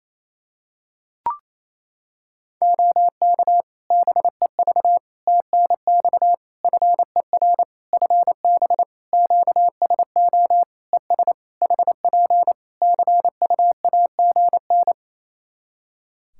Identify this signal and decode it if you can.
Morse code, heard as a keyed tone, about 28 wpm, 700 Hz. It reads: OK BEV TNX FER FB QSO ES HP CUAGN
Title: Practice QSO - Exchange 4 - Ending - 28wpm